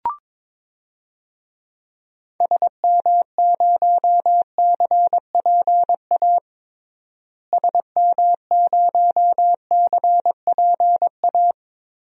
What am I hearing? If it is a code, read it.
Morse code, heard as a keyed tone, about 22 wpm, 700 Hz. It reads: SM0CPA SM0CPA
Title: DXCC-100 Call Signs - 6 Characters - Repeated 3x - 22wpm